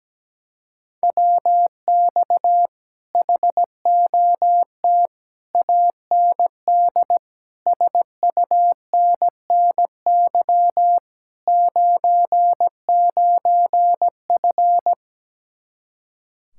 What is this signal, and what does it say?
Morse code, heard as a keyed tone, about 17 wpm, 700 Hz. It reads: WX HOT AND SUNNY 99F
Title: Speed-Racing Common QSO Phrases Encore - 17wpm